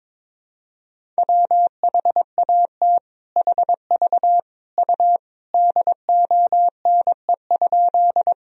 Morse code, heard as a keyed tone, about 21 wpm, 700 Hz. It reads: WHAT HV U DONE?
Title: Sentences from Top 500 Words and Abbreviations 22wpm